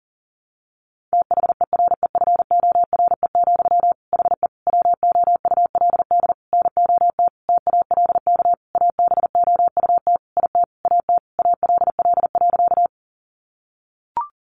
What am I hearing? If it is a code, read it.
Morse code, heard as a keyed tone, about 40 wpm, 700 Hz. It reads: THEREFORE, HE WOULD NOT TALK ABOUT IT AT ALL.
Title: Sentences from Top 800 Words 40wpm